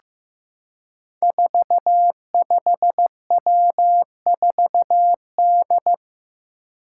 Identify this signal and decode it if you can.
Morse code, heard as a keyed tone, about 15 wpm, 700 Hz. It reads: 45W4D
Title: Five Challenging Letter-Number Combinations 15wpm